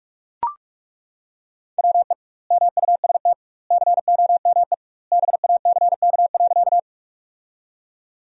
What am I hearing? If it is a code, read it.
Morse code, heard as a keyed tone, about 41 wpm, 700 Hz. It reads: WE MUST COME BACK.
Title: 2x Word-Spacing - Sentences from Top 200 Words - 45wpm